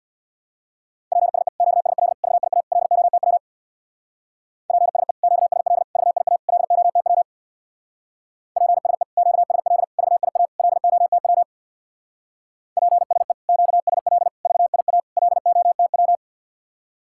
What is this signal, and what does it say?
Morse code, heard as a keyed tone, about 61 wpm, 700 Hz. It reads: PSE QSL VIA LOTW PSE QSL VIA LOTW PSE QSL VIA LOTW PSE QSL VIA LOTW
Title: Speed-Racing Common QSO Phrases - 50wpm